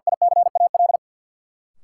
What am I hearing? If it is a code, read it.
Morse code, heard as a keyed tone, about 53 wpm, 700 Hz. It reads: ICAL
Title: Sentences from Top 1100 Words - New Word Drill Alt - 50wpm